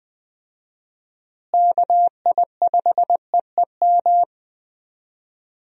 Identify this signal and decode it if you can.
Morse code, heard as a keyed tone, about 20 wpm, 700 Hz. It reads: KI5EEM
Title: USA Call Signs - 6 Characters - Repeated 2x - 20wpm